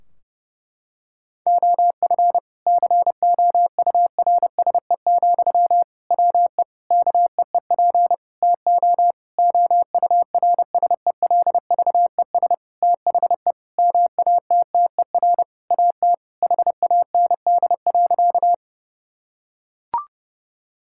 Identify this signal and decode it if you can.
Morse code, heard as a keyed tone, about 32 wpm, 700 Hz. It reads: OF COURSE, WE KEEP TO OURSELVES THE MATTER AT HAND.
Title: Sentences from Top 1600 Words 30wpm